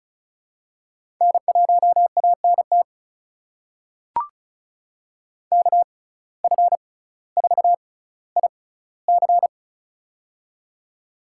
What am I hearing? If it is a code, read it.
Morse code, heard as a keyed tone, about 34 wpm, spaced as for 18 wpm, 700 Hz. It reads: N1ANT KF4IC
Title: USA Call Signs - 5 Characters - 6x Character Spacing - 35wpm